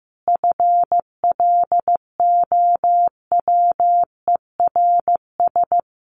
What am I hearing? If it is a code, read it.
Morse code, heard as a keyed tone, about 15 wpm, 700 Hz. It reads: FLOWERS
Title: A Sound Of Thunder 15wpm